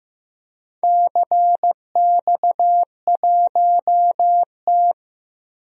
Morse code, heard as a keyed tone, about 15 wpm, 700 Hz. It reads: CX1T
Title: DXCC-100 Call Signs - 4 Characters - Repeated 2x - 15wpm